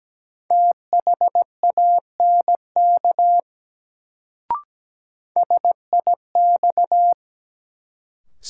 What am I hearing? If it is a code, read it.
Morse code, heard as a keyed tone, about 17 wpm, 700 Hz. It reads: THANK SIX
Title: Sentences from Top 600 Words - New Word Drill - 17wpm